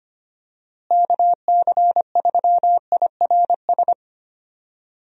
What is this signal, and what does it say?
Morse code, heard as a keyed tone, about 25 wpm, 700 Hz. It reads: KC3IRS
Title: USA Call Signs - 6 Characters - Repeated 3x - 25wpm